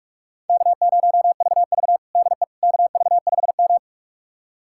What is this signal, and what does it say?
Morse code, heard as a keyed tone, about 46 wpm, 700 Hz. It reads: K0VV DE KV5M
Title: USA Callsign DE Callsign - 4 Characters - 45wpm